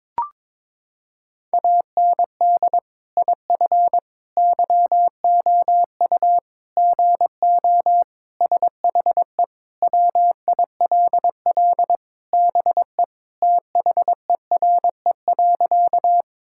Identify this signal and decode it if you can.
Morse code, heard as a keyed tone, about 22 wpm, 700 Hz. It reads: AND IF YOU GO SHE WILL BE THERE.
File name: practice / 22wpm / Sentences with 3-Word Phrases - 22wpm